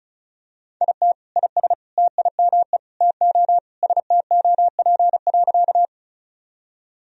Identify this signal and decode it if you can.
Morse code, heard as a keyed tone, about 36 wpm, 700 Hz. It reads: IT IS TIME TO STOP.
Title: Sentences from Top 1000 Words - Review Part 1 - 35wpm